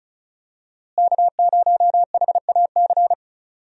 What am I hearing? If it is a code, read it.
Morse code, heard as a keyed tone, about 36 wpm, 700 Hz. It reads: K0HAC
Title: USA Call Signs - 5 Characters - Repeated 3x - 35wpm